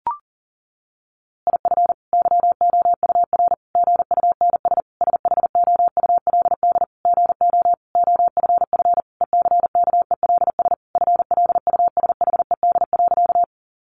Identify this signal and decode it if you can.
Morse code, heard as a keyed tone, about 40 wpm, 700 Hz. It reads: IF YOUR GUNS SHOULD GO OFF ECKELS FLUSHED.
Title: A Sound Of Thunder 40wpm